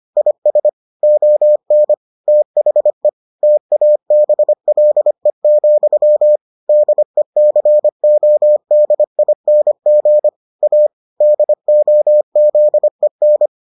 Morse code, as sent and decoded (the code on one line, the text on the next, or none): .. ... / --- -. / - .... / - .- -... .-.. --..-- / -.. -.-. --- -.. .. -. --. / .- / -.. --- --.. -.
IS ON THE TABLE, DECODING A DOZEN